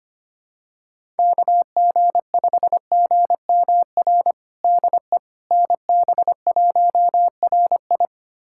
Morse code, heard as a keyed tone, about 25 wpm, 700 Hz. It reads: KG5GMR DE NB1RI
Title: USA Callsign DE Callsign 25wpm